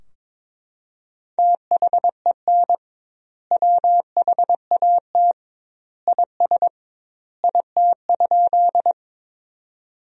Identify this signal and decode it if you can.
Morse code, heard as a keyed tone, about 21 wpm, 700 Hz. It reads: THEN WHAT IS IT?
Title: 2x Word-Spacing - Sentences from Top 100 Words - 22wpm